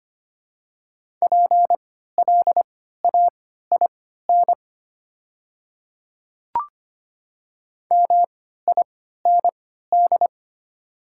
Morse code, heard as a keyed tone, about 25 wpm, spaced as for 14 wpm, 700 Hz. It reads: PLAIN MIND
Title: 3x Character Spacing - Top 500 Words - 25wpm